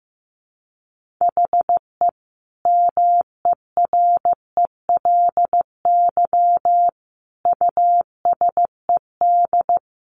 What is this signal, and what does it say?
Morse code, heard as a keyed tone, about 15 wpm, 700 Hz. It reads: HE MERELY USED
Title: Sentences from Top 1400 Words 15wpm